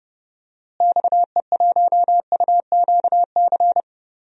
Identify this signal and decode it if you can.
Morse code, heard as a keyed tone, about 30 wpm, 700 Hz. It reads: XE1UQC
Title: DXCC-100 Call Signs - 6 Characters - Repeated 3x - 30wpm